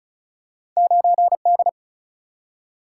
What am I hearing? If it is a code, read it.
Morse code, heard as a keyed tone, about 36 wpm, 700 Hz. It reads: 9D
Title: Single-Letter Single-Number 35wpm